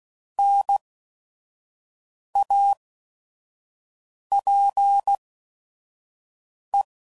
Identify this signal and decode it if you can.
Morse code, heard as a keyed tone, about 16 wpm, spaced as for 5 wpm, 800 Hz. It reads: NAPE
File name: Prefixes / By First Letter / Ns